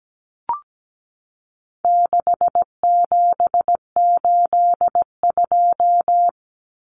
Morse code, heard as a keyed tone, about 17 wpm, 700 Hz. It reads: 6782
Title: Common Cut Numbers - 4-Digit 17wpm